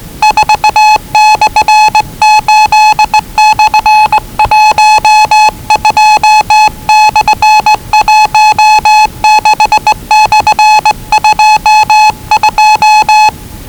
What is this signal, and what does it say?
Morse code, heard as a keyed tone, about 20 wpm, 840 Hz. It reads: 4/8/12/16/22
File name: WellingtonRadioZlwCloseDownOnHfAndMf / 01-Zlw 4277khz 417.5 qrt until 0925z